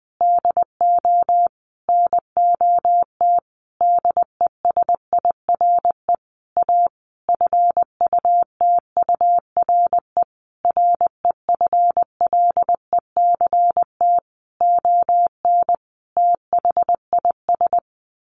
Morse code, heard as a keyed tone, about 20 wpm, 700 Hz. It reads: DO NOT DESIRE A FUTURE REFLECT ON THIS
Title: Sentences from Top 1500 Words 20wpm